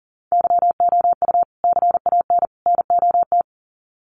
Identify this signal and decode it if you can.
Morse code, heard as a keyed tone, about 39 wpm, 700 Hz. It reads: YOU CAN NOT
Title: Sets of 3 Words 40wpm